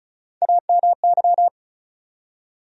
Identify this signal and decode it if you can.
Morse code, heard as a keyed tone, about 36 wpm, 700 Hz. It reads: AMY
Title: USA Common Names - Max 5 Letters - 35wpm